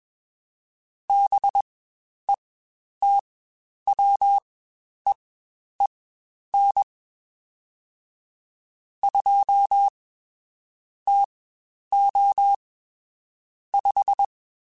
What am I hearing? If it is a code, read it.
Morse code, heard as a keyed tone, about 21 wpm, spaced as for 10 wpm, 800 Hz. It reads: BETWEEN 2 TO 5